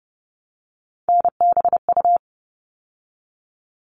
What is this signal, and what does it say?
Morse code, heard as a keyed tone, about 30 wpm, 700 Hz. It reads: NBU
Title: All Cut Numbers - 3-Digit 30wpm